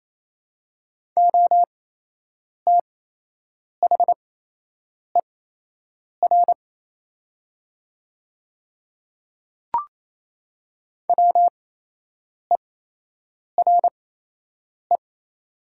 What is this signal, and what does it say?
Morse code, heard as a keyed tone, about 28 wpm, spaced as for 8 wpm, 700 Hz. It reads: OTHER WERE
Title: 8x Character Spacing - Top 100 Words - 28wpm